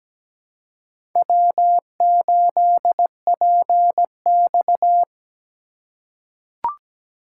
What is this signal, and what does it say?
Morse code, heard as a keyed tone, about 17 wpm, 700 Hz. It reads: W8PX
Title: USA Call Signs - 4 Characters - 3x Character Spacing - 17wpm